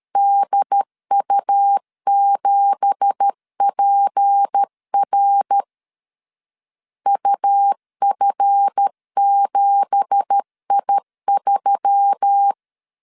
Morse code, heard as a keyed tone, about 13 wpm, 800 Hz. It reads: DU7PR UF7I3